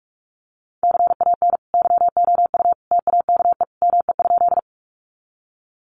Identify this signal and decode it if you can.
Morse code, heard as a keyed tone, about 44 wpm, 700 Hz. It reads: CAN YOU TAKE ME?
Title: Sentences from Top 200 Words 45wpm